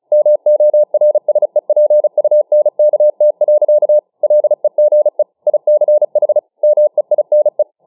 Morse code, heard as a keyed tone, about 35 wpm, 600 Hz. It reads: MORSEPUNKT. LEGE ICH MEINE